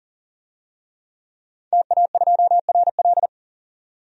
Morse code, heard as a keyed tone, about 37 wpm, 700 Hz. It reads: TA2RL
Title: DXCC-100 Call Signs - 5 Characters - 40wpm